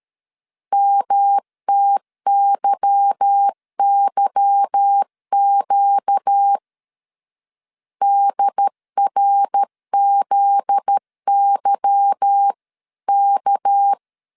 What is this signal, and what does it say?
Morse code, heard as a keyed tone, about 13 wpm, 800 Hz. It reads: MTYYQ DRZYK
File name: LEZIONI / 8 YJ45 / 8 Y 12wpm 2228